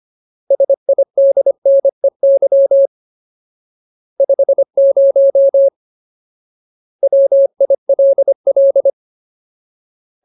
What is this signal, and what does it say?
Morse code, heard as a keyed tone, about 25 wpm, 550 Hz. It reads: SIDNEY 50 WILL
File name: Ultimate Word List / 25WPM Ultimate